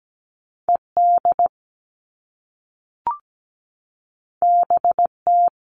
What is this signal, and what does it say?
Morse code, heard as a keyed tone, about 17 wpm, 700 Hz. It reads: ED BT
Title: All Cut Numbers - 2-Digit 17wpm